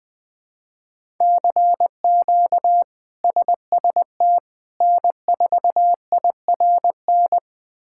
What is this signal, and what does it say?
Morse code, heard as a keyed tone, about 20 wpm, 700 Hz. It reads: CQ SST N4IRN
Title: K1USN SST QSO Practice - 20wpm